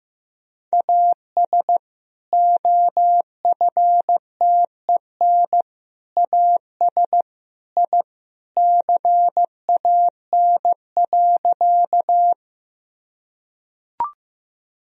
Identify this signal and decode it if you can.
Morse code, heard as a keyed tone, about 15 wpm, 700 Hz. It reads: AS OFTEN AS I CAN.
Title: Sentences from Top 1000 Words - Review Part 1 - 15wpm